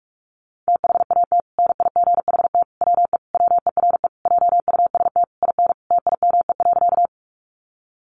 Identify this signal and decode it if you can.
Morse code, heard as a keyed tone, about 44 wpm, 700 Hz. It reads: THAT NIGHT WE WERE JUST IN TIME.
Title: Sentences from Top 300 Words 45wpm